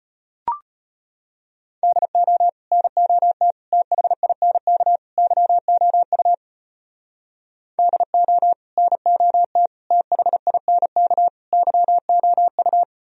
Morse code, heard as a keyed tone, about 33 wpm, 700 Hz. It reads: DO NOT THINK YOU DO NOT THINK YOU